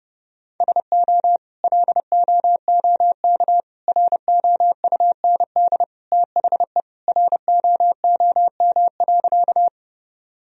Morse code, heard as a keyed tone, about 30 wpm, 700 Hz. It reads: SO LOOK ROUND THE ROOM.
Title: Sentences from Top 1000 Words - Review Part 1 - 30wpm